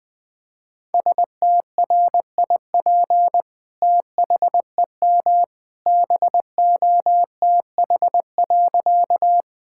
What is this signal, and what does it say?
Morse code, heard as a keyed tone, about 20 wpm, 700 Hz. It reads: STRIP THEM BOTH.
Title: Sentences from Top 1600 Words 20wpm